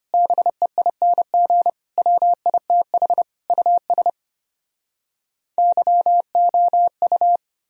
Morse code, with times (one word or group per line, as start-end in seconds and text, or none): 0.1-1.7 BEING
2.0-3.2 WITH
3.5-4.1 US
5.6-7.4 YOU